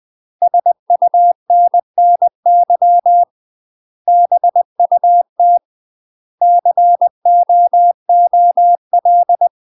Morse code, heard as a keyed tone, about 20 wpm, 700 Hz. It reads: SUNNY BUT COOL
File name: UK MASTER QSOs TRAINING / 20WPM / UK MASTER QSOs 0003